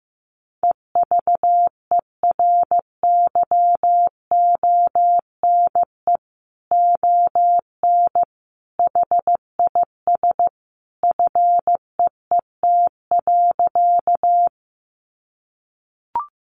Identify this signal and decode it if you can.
Morse code, heard as a keyed tone, about 15 wpm, 700 Hz. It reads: EVERYONE ON HIS FEET.
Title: A Sound Of Thunder 15wpm